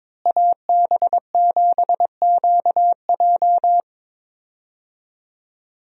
Morse code, heard as a keyed tone, about 22 wpm, 700 Hz. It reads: AB7QJ
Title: USA Call Signs - 5 Characters - Repeated 2x - 22wpm